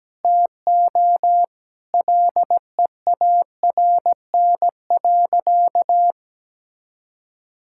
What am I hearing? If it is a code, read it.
Morse code, heard as a keyed tone, about 17 wpm, 700 Hz. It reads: TO LEARN.
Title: Sentences from Top 1000 Words - Review Part 6 - 17wpm